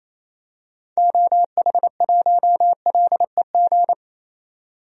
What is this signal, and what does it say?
Morse code, heard as a keyed tone, about 28 wpm, 700 Hz. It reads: OH1LEG
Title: DXCC-100 Call Signs - 6 Characters - 28wpm